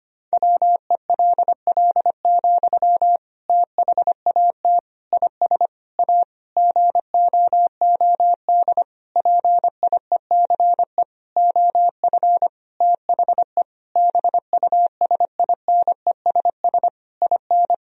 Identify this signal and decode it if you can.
Morse code, heard as a keyed tone, about 25 wpm, 700 Hz. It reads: WELL, THAT IS A GOOD PIECE OF THE BUSINESS IN
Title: Sentences from Top 1000 Words - Review Part 2 - 25wpm